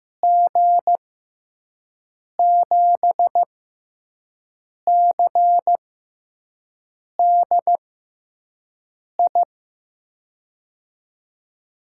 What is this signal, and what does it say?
Morse code, heard as a keyed tone, about 15 wpm, spaced as for 5 wpm, 700 Hz. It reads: G7CDI
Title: USA Call Signs - 6 Characters - 6x Character Spacing - 15wpm